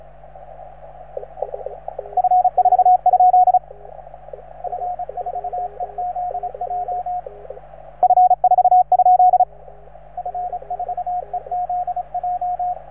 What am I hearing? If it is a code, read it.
Morse code, heard as a keyed tone, about 34 wpm, 720 Hz, with radio noise. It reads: F4? F4EZJ F4? F4EZJ